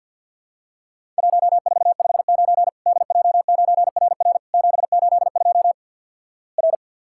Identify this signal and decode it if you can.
Morse code, heard as a keyed tone, about 50 wpm, 700 Hz. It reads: 1459 DJ9RR 782 R
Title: Worked All Europe DX Contest - QTC Exchange - 50wpm